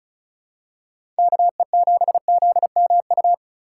Morse code, heard as a keyed tone, about 36 wpm, 700 Hz. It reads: KE7ZMU
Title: USA Call Signs - 6 Characters - 2x Character Spacing - 35wpm